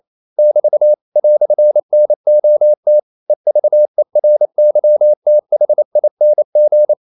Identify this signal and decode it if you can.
Morse code, heard as a keyed tone, about 28 wpm, 600 Hz. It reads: <BT> "NOT EVERYTHING
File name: CW / Practice / EnglishQuotations0128